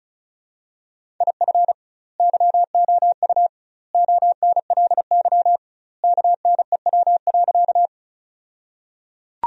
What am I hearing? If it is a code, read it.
Morse code, heard as a keyed tone, about 36 wpm, 700 Hz. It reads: IF YOU ONLY KNEW.
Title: 2x Word-Spacing - Sentences from Top 400 Words - 35wpm